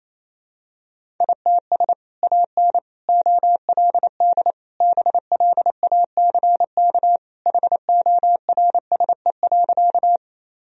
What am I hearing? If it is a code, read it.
Morse code, heard as a keyed tone, about 28 wpm, 700 Hz. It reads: ITS AN OLD BLACK HORSE.
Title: Sentences from Top 1000 Words - Review Part 2 - 28wpm